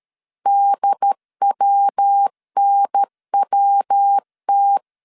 Morse code, heard as a keyed tone, about 13 wpm, 800 Hz. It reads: DWNWT
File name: CW / DET / 1 3 1500C S C 12wpm 2750